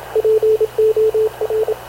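Morse code, heard as a keyed tone, about 27 wpm, 440 Hz, with radio noise. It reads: POR
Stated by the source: Havana Radio / CLA sending Spanish press (PX) in Morse, recorded by Radio Officer Paul Dumesnil onboard CNAV QUEST/CZDO.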